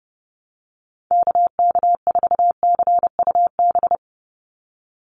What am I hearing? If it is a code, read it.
Morse code, heard as a keyed tone, about 28 wpm, 700 Hz. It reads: KK4CUB
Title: USA Call Signs 30wpm